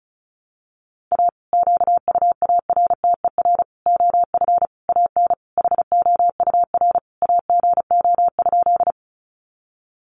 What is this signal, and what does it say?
Morse code, heard as a keyed tone, about 34 wpm, 700 Hz. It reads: A QUARTER OF AN HOUR AGO?
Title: Sentences from Top 1300 Words 35wpm